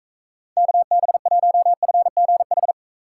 Morse code, heard as a keyed tone, about 44 wpm, 700 Hz. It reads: KB1FGH